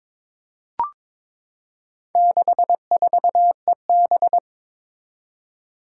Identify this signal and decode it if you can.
Morse code, heard as a keyed tone, about 22 wpm, 700 Hz. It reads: 64EB
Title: Four Challenging Letter-Number Combinations 22wpm